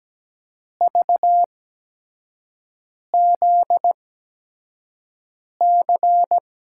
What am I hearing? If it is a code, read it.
Morse code, heard as a keyed tone, about 17 wpm, spaced as for 5 wpm, 700 Hz. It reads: VZC
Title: USA Call Signs - 6 Characters - 8x Character Spacing - 17wpm